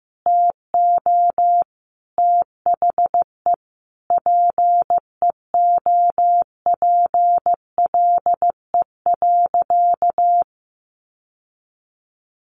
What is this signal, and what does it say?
Morse code, heard as a keyed tone, about 15 wpm, 700 Hz. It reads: TO THE PEOPLE.
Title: Sentences from Top 1500 Words 15wpm